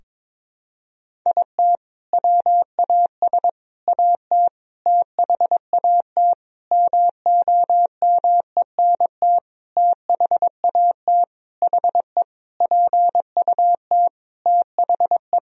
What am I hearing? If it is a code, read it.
Morse code, heard as a keyed tone, about 22 wpm, 700 Hz. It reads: IT WAS AT THAT MOMENT THAT HE PUT THE
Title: Sentences from Top 1000 Words - Review Part 3 - 22wpm